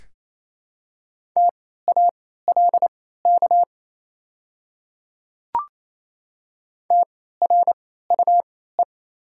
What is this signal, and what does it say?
Morse code, heard as a keyed tone, about 28 wpm, spaced as for 16 wpm, 690 Hz. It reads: TALK TRUE
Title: 3x Character Spacing - Top 500 Words - 28wpm